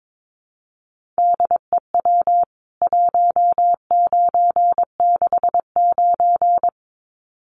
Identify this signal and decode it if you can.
Morse code, heard as a keyed tone, about 22 wpm, 700 Hz. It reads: DEW 1969
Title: CWT Exchange 22wpm